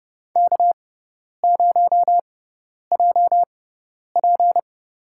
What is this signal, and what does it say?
Morse code, heard as a keyed tone, about 30 wpm, spaced as for 10 wpm, 700 Hz. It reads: K0JP
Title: USA Call Signs - 4 Characters - 6x Character Spacing - 30wpm